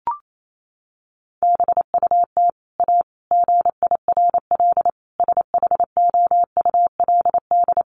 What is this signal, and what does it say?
Morse code, heard as a keyed tone, about 28 wpm, 700 Hz. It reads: BUT A GIRL SHOULD
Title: Sentences from Top 1200 Words 28wpm